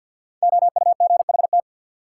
Morse code, heard as a keyed tone, about 51 wpm, 700 Hz. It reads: OUGHT
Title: Sets of 5 Words Encore - New Word Drill - 50wpm